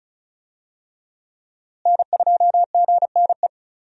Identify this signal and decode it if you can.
Morse code, heard as a keyed tone, about 36 wpm, 700 Hz. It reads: N2GNE
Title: USA Call Signs - 5 Characters - Repeated 3x - 35wpm